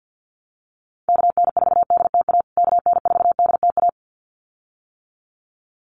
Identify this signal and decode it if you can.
Morse code, heard as a keyed tone, about 50 wpm, 700 Hz. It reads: KN4DTA KN4DTA
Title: USA Call Signs Repeated 2x 50wpm